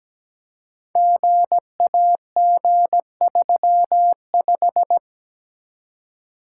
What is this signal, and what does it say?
Morse code, heard as a keyed tone, about 17 wpm, 700 Hz. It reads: GAG35
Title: Code Groups - 5 Letter:Numbers - 17wpm